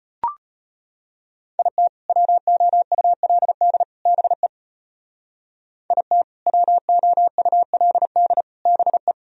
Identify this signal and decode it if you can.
Morse code, heard as a keyed tone, about 30 wpm, 700 Hz. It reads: IT WOULD BE IT WOULD BE
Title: Speed-Racing - Sets of 3 Words 25wpm